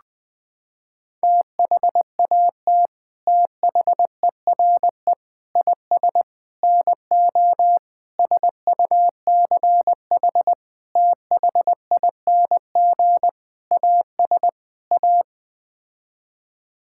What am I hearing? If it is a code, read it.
Morse code, heard as a keyed tone, about 20 wpm, 700 Hz. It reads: THAT THERE IS NO SUCH THING AS A
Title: Sets of 8 Words - 20wpm